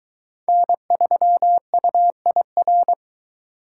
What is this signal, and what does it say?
Morse code, heard as a keyed tone, about 23 wpm, 700 Hz. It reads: N3UIR